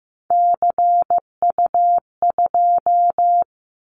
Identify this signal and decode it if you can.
Morse code, heard as a keyed tone, about 15 wpm, 700 Hz. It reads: CU2
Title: Callsign Prefixes for Top 50 DXCC - 15wpm